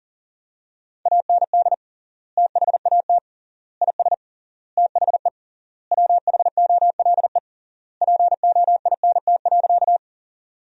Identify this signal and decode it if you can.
Morse code, heard as a keyed tone, about 40 wpm, 700 Hz. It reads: AND THAT IS THE WHOLE POINT.
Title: 3x Word-Spacing - Sentences from Top 400 Words - 40wpm